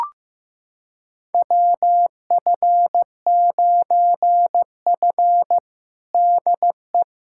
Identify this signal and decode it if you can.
Morse code, heard as a keyed tone, about 15 wpm, 700 Hz. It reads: WF9F DE
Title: USA Callsign DE Callsign - 4 Characters - 15wpm